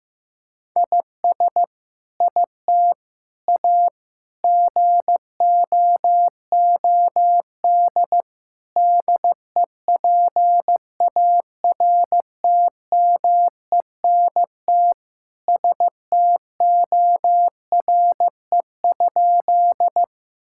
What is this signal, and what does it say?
Morse code, heard as a keyed tone, about 15 wpm, 700 Hz. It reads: IS IT A GOOD DEPARTMENT STORE?
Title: Sentences from Top 1000 Words - Review Part 5 - 15wpm